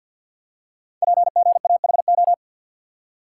Morse code, heard as a keyed tone, about 48 wpm, 700 Hz. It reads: PGAHO
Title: Code Groups - 5 Letters - 50wpm